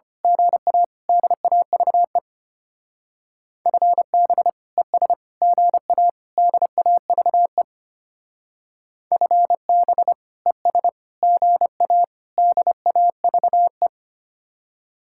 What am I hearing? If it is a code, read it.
Morse code, heard as a keyed tone, about 30 wpm, 700 Hz. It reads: GA DAVE FB ES GA DAVE FB ES GA DAVE
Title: Speed-Racing Common QSO Phrases Encore - 25wpm